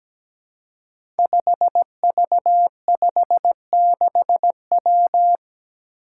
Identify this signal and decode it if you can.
Morse code, heard as a keyed tone, about 17 wpm, 700 Hz. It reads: 5V56W
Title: Five Challenging Letter-Number Combinations 17wpm